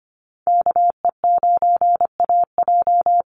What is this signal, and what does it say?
Morse code, heard as a keyed tone, about 25 wpm, 700 Hz. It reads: KE9AJ
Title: NA SOTA Chaser Callsigns 25wpm